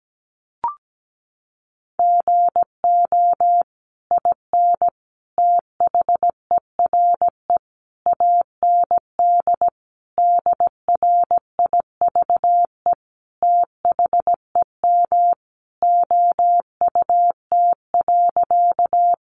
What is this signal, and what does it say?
Morse code, heard as a keyed tone, about 17 wpm, 700 Hz. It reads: GO IN THERE AND DRIVE THEM OUT.
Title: Sentences from Top 500 Words 17wpm